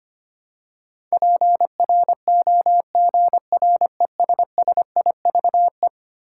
Morse code, heard as a keyed tone, about 25 wpm, 700 Hz. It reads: PROGRESSIVE
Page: morse-code-ninja-practice-25wpm